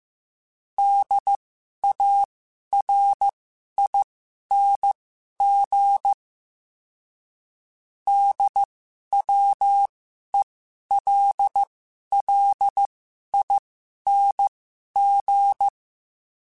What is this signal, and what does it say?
Morse code, heard as a keyed tone, about 15 wpm, spaced as for 11 wpm, 800 Hz. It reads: DARING DWELLING
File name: Special INGs-mp3s / Special INGs-mp3s / Sngl Ing @10